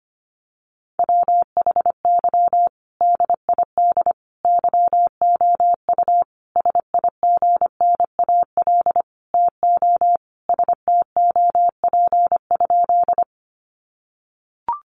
Morse code, heard as a keyed tone, about 25 wpm, 700 Hz. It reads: WHY DID YOU SIGNAL TO STOP?
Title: Sentences from Top 1400 Words 25wpm